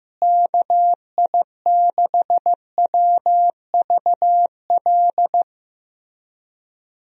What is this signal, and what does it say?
Morse code, heard as a keyed tone, about 15 wpm, 700 Hz. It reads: KI6WVL
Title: USA Call Signs - 6 Characters - Repeated 3x - 15wpm